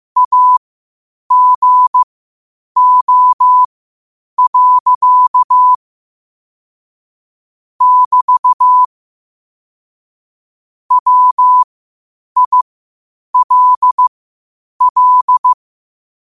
Morse code, heard as a keyed tone, about 15 wpm, spaced as for 8 wpm, 1000 Hz. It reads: AGO. <BT> WILL
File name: qotdfiles / 8wpm